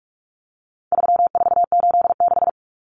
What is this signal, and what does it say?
Morse code, heard as a keyed tone, about 44 wpm, 700 Hz. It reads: 3486
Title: Common Cut Numbers - 4-Digit 45wpm